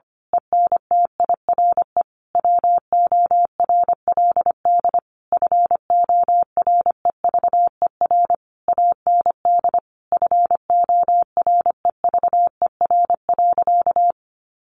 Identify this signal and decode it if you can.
Morse code, heard as a keyed tone, about 25 wpm, 700 Hz. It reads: ENTIRE WORLD FOREVER AND FOREVER.
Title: A Sound Of Thunder 25wpm